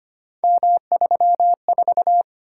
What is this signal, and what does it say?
Morse code, heard as a keyed tone, about 25 wpm, 700 Hz. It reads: M34
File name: Three Letter-Number Combinations - 25wpm